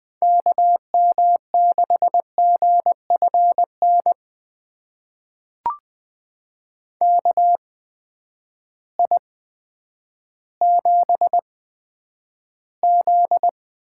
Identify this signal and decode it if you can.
Morse code, heard as a keyed tone, about 20 wpm, spaced as for 10 wpm, 700 Hz. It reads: KM6GFN KI7Z
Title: USA Call Signs - 6 Characters - 8x Character Spacing - 20wpm